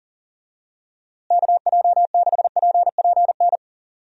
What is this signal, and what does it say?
Morse code, heard as a keyed tone, about 40 wpm, 700 Hz. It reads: KJ6PPN
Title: USA Call Signs - 6 Characters - Repeated 2x - 40wpm